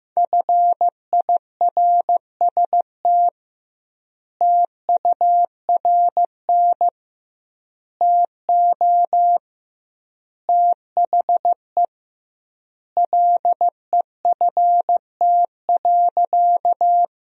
Morse code, heard as a keyed tone, about 15 wpm, 700 Hz. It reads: FIRST TURN TO THE LEFT.
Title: 2x Word-Spacing - Sentences from Top 300 Words - 15wpm